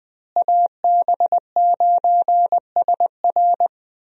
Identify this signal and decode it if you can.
Morse code, heard as a keyed tone, about 20 wpm, 700 Hz. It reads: AB9SR
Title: USA Call Signs - 5 Characters - Repeated 2x - 20wpm